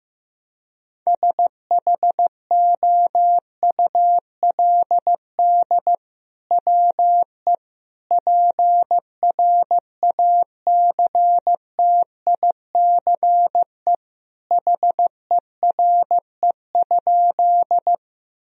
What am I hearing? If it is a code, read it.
Morse code, heard as a keyed tone, about 15 wpm, 700 Hz. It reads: SHOULD WE PRACTICE HERE?
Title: Sentences from Top 1000 Words - Review Part 4 - 15wpm